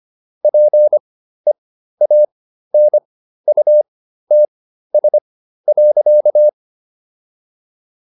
Morse code, heard as a keyed tone, about 25 wpm, spaced as for 13 wpm, 600 Hz. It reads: PEANUTS.
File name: SS109 13